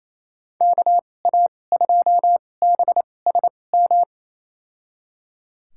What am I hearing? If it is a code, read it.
Morse code, heard as a keyed tone, about 28 wpm, spaced as for 20 wpm, 700 Hz. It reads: KA2BSM
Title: USA Call Signs - 6 Characters - 2x Character Spacing - 28wpm